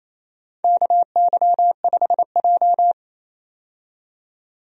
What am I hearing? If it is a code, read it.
Morse code, heard as a keyed tone, about 28 wpm, 700 Hz. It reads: KY5J
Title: USA Call Signs - 4 Characters - 28wpm